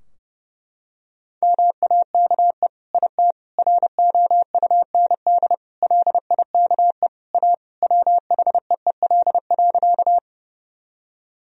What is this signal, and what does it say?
Morse code, heard as a keyed tone, about 30 wpm, 700 Hz. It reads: MAKE IT ROUND LIKE A WHEEL.
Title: Sentences from Top 1000 Words - Review Part 3 - 30wpm